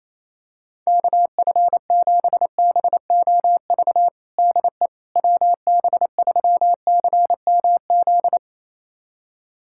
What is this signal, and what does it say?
Morse code, heard as a keyed tone, about 28 wpm, 700 Hz. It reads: KF7BOV DE WB3CMZ
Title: USA Callsign DE Callsign 28wpm